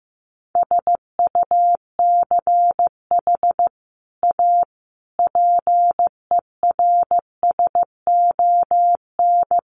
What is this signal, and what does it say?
Morse code, heard as a keyed tone, about 15 wpm, 700 Hz. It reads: SUCH A PERSON
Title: Sentences from Top 1400 Words 15wpm